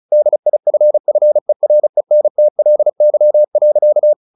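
Morse code, heard as a keyed tone, about 34 wpm, 600 Hz. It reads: DIFFERENTLY.
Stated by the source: Morse practice at 35 wpm from CW med Gnister, sending Ken Follett's novel Never in English.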